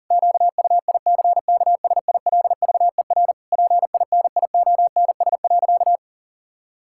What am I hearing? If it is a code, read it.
Morse code, heard as a keyed tone, about 40 wpm, 700 Hz. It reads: QUICKSILVER PINIONS.
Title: Pangram Challenge v2 - 40wpm